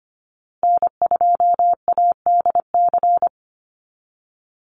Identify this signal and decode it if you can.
Morse code, heard as a keyed tone, about 25 wpm, 700 Hz. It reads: N2ADC
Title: Call Signs Repeated 3x - 25wpm